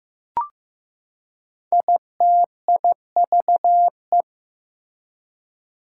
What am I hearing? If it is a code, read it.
Morse code, heard as a keyed tone, about 15 wpm, 700 Hz. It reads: ITIVE
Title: Prefixes and Suffixes 15wpm